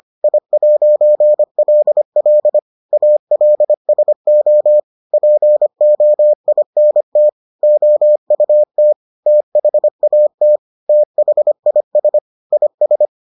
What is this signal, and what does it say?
Morse code, heard as a keyed tone, about 25 wpm, 600 Hz. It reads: I'LL ALSO POINT OUT THAT THIS IS